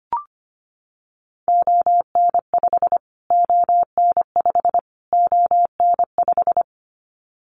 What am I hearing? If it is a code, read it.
Morse code, heard as a keyed tone, about 25 wpm, 700 Hz. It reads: ON5 ON5 ON5
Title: Callsign Prefixes for Top 25 DXCC - Repeated 3x - 25wpm